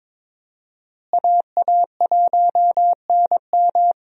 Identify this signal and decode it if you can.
Morse code, heard as a keyed tone, about 22 wpm, 700 Hz. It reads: AA1NM
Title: USA Call Signs - 5 Characters - Repeated 3x - 22wpm